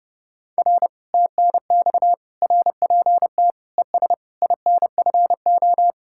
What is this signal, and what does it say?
Morse code, heard as a keyed tone, about 37 wpm, 700 Hz. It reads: R TNX RPT ES INFO
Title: Speed-Racing Common QSO Phrases - 30wpm